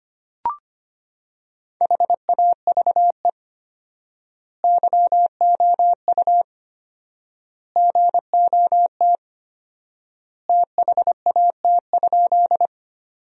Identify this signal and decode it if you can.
Morse code, heard as a keyed tone, about 25 wpm, 700 Hz. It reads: HAVE YOU GOT THAT?
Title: 4x Word-Spacing - Sentences from Top 300 Words - 25wpm